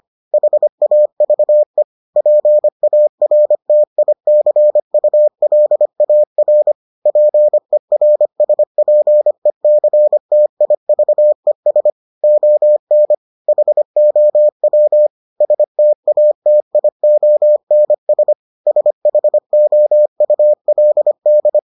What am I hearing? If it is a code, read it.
Morse code, heard as a keyed tone, about 25 wpm, 600 Hz. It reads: HAVE PARTICULAR PERSPECTIVES ON HOW STATIONS SHOULD